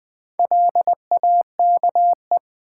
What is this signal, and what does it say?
Morse code, heard as a keyed tone, about 20 wpm, 700 Hz. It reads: LAKE
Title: Sentences from Top 1100 Words - New Word Drill Alt v2 - 20wpm